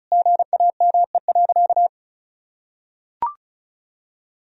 Morse code, heard as a keyed tone, about 34 wpm, 700 Hz. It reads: GAME.
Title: Sentences from Top 1000 Words - Review Part 4 - 35wpm